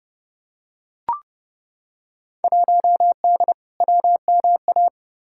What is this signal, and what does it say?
Morse code, heard as a keyed tone, about 32 wpm, 700 Hz. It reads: 1D WMA
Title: Field Day Exchanges - 30wpm